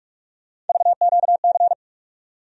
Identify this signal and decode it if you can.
Morse code, heard as a keyed tone, about 41 wpm, 700 Hz. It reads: VQC
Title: USA Call Sign Postfixes - 3 Letters 45wpm